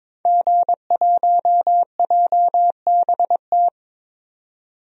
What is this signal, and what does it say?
Morse code, heard as a keyed tone, about 22 wpm, 700 Hz. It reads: G1JBT
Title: DXCC-100 Call Signs - 5 Characters - 22wpm